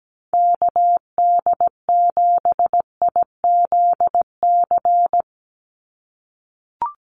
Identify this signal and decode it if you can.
Morse code, heard as a keyed tone, about 17 wpm, 700 Hz. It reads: KD7IZC
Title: USA Call Signs 17wpm